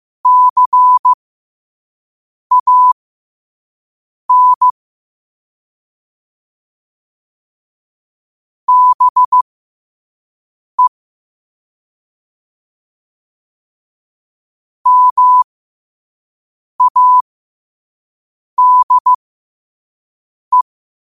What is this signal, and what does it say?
Morse code, heard as a keyed tone, about 15 wpm, spaced as for 5 wpm, 1000 Hz. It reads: CAN BE MADE